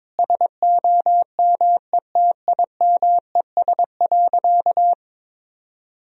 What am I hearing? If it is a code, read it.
Morse code, heard as a keyed tone, about 22 wpm, 700 Hz. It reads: SOMETIMES.
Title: Sentences from Top 1000 Words - Review Part 3 - 22wpm